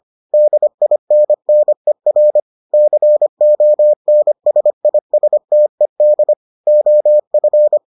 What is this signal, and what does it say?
Morse code, heard as a keyed tone, about 25 wpm, 600 Hz. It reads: DINNER CONSISTED OF